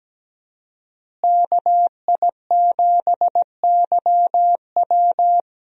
Morse code, heard as a keyed tone, about 17 wpm, 700 Hz. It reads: KI7YW
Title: USA Call Signs - 5 Characters - Repeated 3x - 17wpm